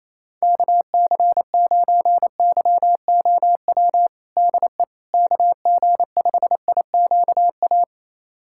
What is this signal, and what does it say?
Morse code, heard as a keyed tone, about 28 wpm, 700 Hz. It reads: KC9YOW DE KG5IQA
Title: USA Callsign DE Callsign - 6 Characters - 28wpm